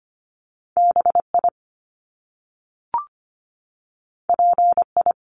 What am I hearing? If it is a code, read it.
Morse code, heard as a keyed tone, about 25 wpm, 700 Hz. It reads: BI PI
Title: Top Two Letter Combinations in Words 25wpm